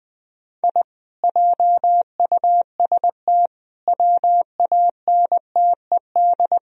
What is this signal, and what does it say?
Morse code, heard as a keyed tone, about 20 wpm, 700 Hz. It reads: I JUST WANTED